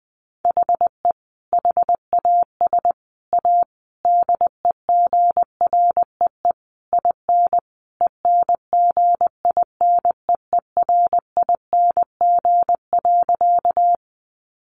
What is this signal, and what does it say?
Morse code, heard as a keyed tone, about 20 wpm, 700 Hz. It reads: HE HAS A DEGREE IN ENGINEERING.
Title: Sentences from Top 1500 Words 20wpm